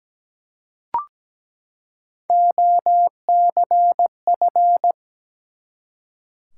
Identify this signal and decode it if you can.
Morse code, heard as a keyed tone, about 17 wpm, 700 Hz. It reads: OCF
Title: Top 180 QSO Elements 17wpm